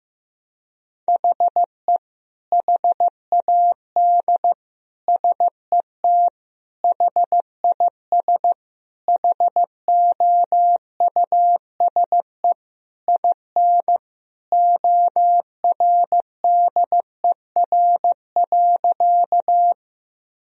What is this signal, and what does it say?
Morse code, heard as a keyed tone, about 15 wpm, 700 Hz. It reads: HE HAD SET HIS HOUSE IN ORDER.
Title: Sentences from Top 1000 Words - Review Part 2 - 15wpm